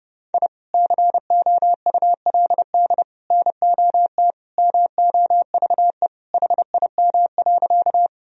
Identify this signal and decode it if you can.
Morse code, heard as a keyed tone, about 30 wpm, 700 Hz. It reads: I COULD NOT MOVE HIM.
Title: Sentences from Top 1000 Words - Review Part 1 - 30wpm